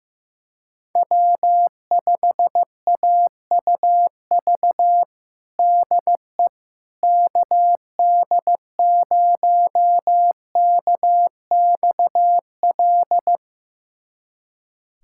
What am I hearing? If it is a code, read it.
Morse code, heard as a keyed tone, about 15 wpm, 700 Hz. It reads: W5AUV DE KD0KXL
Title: USA Callsign DE Callsign 15wpm